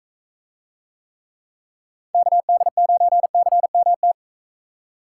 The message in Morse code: KD9CMT

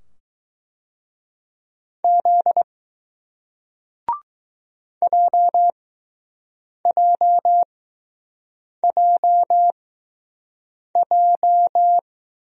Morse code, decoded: Z JJJJ

